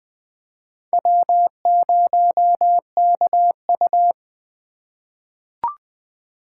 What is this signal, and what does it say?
Morse code, heard as a keyed tone, about 20 wpm, 700 Hz. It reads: W0KU
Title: USA Call Signs - 4 Characters - 3x Character Spacing - 20wpm